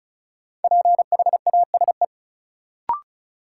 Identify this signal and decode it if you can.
Morse code, heard as a keyed tone, about 36 wpm, 700 Hz. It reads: PHASE
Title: Sentences from Top 1200 Words - New Word Drill - 35wpm